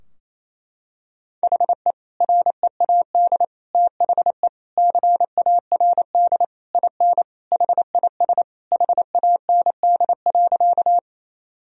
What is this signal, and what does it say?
Morse code, heard as a keyed tone, about 28 wpm, 700 Hz. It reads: HE READ THE CARD IN HIS HAND.
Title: Sentences from Top 1000 Words - Review Part 4 - 28wpm